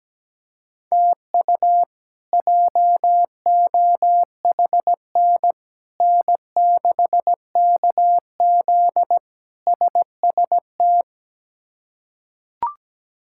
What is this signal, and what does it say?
Morse code, heard as a keyed tone, about 17 wpm, 700 Hz. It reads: TU JOHN N6KZ SST
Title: K1USN SST QSO Practice - 17wpm